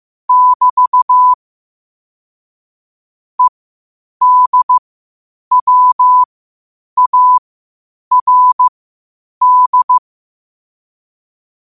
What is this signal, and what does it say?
Morse code, heard as a keyed tone, about 15 wpm, spaced as for 8 wpm, 1000 Hz. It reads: <BT> EDWARD
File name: qotdfiles / 8wpm